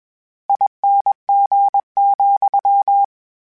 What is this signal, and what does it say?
Morse code, heard as a keyed tone, about 21 wpm, 800 Hz. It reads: ING,
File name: General / General-mp3s / ESE ING @20